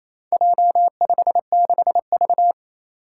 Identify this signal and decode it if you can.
Morse code, heard as a keyed tone, about 27 wpm, 700 Hz. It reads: J56V
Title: Four Challenging Letter-Number Combinations 28wpm